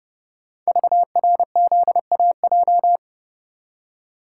Morse code, heard as a keyed tone, about 29 wpm, 700 Hz. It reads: VRZAJ
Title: Code Groups - 5 Letters - 30wpm